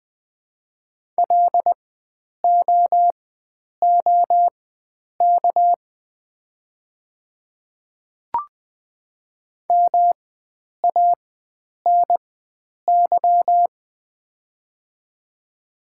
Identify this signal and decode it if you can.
Morse code, heard as a keyed tone, about 21 wpm, spaced as for 9 wpm, 700 Hz. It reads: LOOK MANY
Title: 4x Character Spacing - Top 100 Words - 20wpm